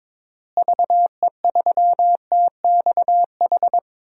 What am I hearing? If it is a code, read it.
Morse code, heard as a keyed tone, about 22 wpm, 700 Hz. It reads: VE3TXH
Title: DXCC-100 Call Signs - 6 Characters - Repeated 3x - 22wpm